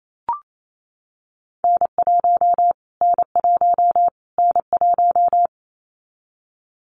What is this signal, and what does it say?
Morse code, heard as a keyed tone, about 28 wpm, 700 Hz. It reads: N1 N1 N1
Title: Callsign Prefixes for Top 25 DXCC - Repeated 3x - 28wpm